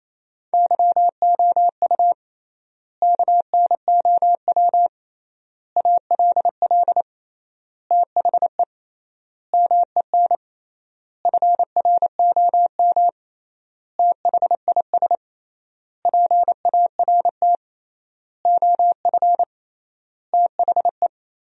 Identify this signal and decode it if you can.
Morse code, heard as a keyed tone, about 28 wpm, 700 Hz. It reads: YOU KNOW ALL THE MEN FROM THIS PART OF THE
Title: 3x Word-Spacing - Sentences from Top 300 Words - 28wpm